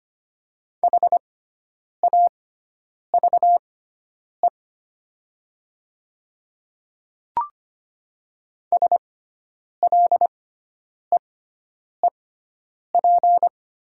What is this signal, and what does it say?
Morse code, heard as a keyed tone, about 25 wpm, spaced as for 9 wpm, 700 Hz. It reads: HAVE SLEEP
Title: 6x Character Spacing - Top 500 Words - 25wpm